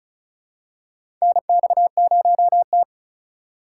Morse code, heard as a keyed tone, about 36 wpm, 700 Hz. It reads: NX0T
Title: USA Call Signs - 4 Characters - 4x Character Spacing - 35wpm